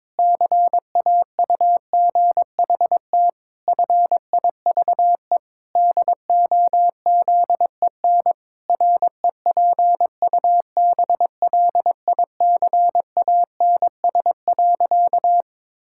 Morse code, heard as a keyed tone, about 22 wpm, 700 Hz. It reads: CAUGHT FIVE DOZEN REPUBLICANS.
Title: Pangram Challenge v2 - 22wpm